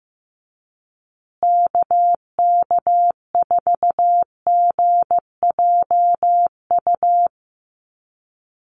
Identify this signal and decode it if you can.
Morse code, heard as a keyed tone, about 15 wpm, 700 Hz. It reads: KK4GJU
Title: USA Call Signs Repeated 3x 15wpm